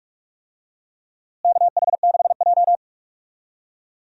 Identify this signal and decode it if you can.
Morse code, heard as a keyed tone, about 44 wpm, 700 Hz. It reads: KH6J
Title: DXCC-100 Call Signs - 4 Characters - Repeated 2x - 45wpm